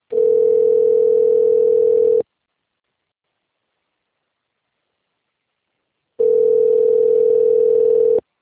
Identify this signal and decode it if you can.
Ringback tone (North America), 440 + 480 Hz.